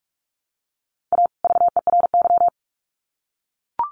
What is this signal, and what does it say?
Morse code, heard as a keyed tone, about 44 wpm, 700 Hz. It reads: A VERY
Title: Sets of 2 Words 45wpm